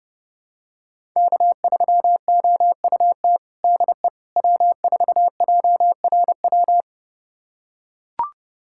Morse code, heard as a keyed tone, about 32 wpm, 700 Hz. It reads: K3OUT DE W4JRW
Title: USA Callsign DE Callsign - 5 Characters - 30wpm